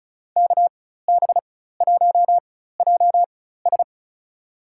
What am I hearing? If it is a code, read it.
Morse code, heard as a keyed tone, about 36 wpm, spaced as for 16 wpm, 700 Hz. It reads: KB1JS
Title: USA Call Signs - 5 Characters - 4x Character Spacing - 35wpm